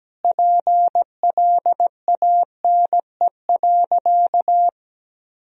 Morse code, heard as a keyed tone, about 17 wpm, 700 Hz. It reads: PLANE.
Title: Sentences from Top 1000 Words - Review Part 3 - 17wpm